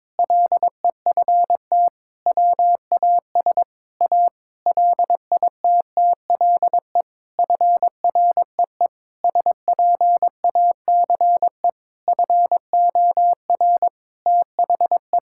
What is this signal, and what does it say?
Morse code, heard as a keyed tone, about 21 wpm, 700 Hz. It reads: LEFT WAS A LITTLE FREE SPACE FOR THE
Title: Sentences from Top 1000 Words - Review Part 5 - 22wpm